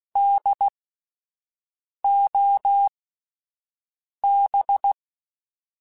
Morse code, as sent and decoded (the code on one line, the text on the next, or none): -.. --- -...
DOB